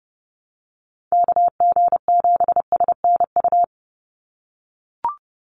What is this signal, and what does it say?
Morse code, heard as a keyed tone, about 30 wpm, 700 Hz. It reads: KG7SNU
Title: USA Call Signs 30wpm